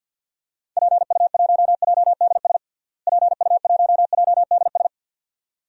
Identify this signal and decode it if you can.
Morse code, heard as a keyed tone, about 53 wpm, 700 Hz. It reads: PU1JDS PU1JDS
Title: DXCC-100 Call Signs - 6 Characters - Repeated 2x - 50wpm